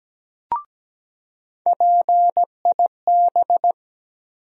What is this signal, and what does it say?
Morse code, heard as a keyed tone, about 17 wpm, 700 Hz. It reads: PIB
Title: USA Call Sign Postfixes - 3 Letters 17wpm